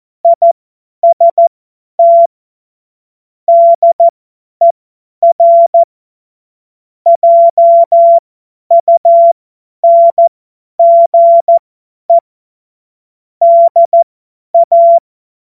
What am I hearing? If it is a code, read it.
Morse code, heard as a keyed tone, about 14 wpm, spaced as for 10 wpm, 680 Hz. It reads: IST DER JUNGE DA
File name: Morsekurs / Witze / Witz 07. Dezember 2021, 50 F70